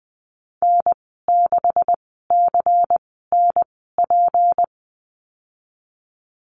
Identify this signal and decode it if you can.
Morse code, heard as a keyed tone, about 20 wpm, spaced as for 15 wpm, 700 Hz. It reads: N6CNP
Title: 154 Lesson 25g - Letter p - Call Signs 20wpm